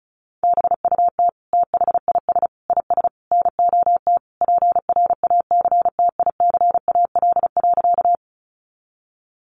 Morse code, heard as a keyed tone, about 34 wpm, 700 Hz. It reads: BUT THIS IS NOT PRACTICAL.
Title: Sentences from Top 1400 Words 35wpm